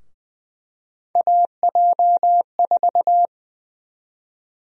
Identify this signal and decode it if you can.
Morse code, heard as a keyed tone, about 20 wpm, 700 Hz. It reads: AJ4
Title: USA Call Sign Prefixes - Repeated 2x - 20wpm